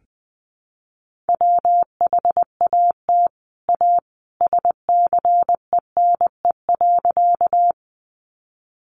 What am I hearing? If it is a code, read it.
Morse code, heard as a keyed tone, about 20 wpm, 700 Hz. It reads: WHAT A SCENE.
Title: Sentences from Top 1100 Words 20wpm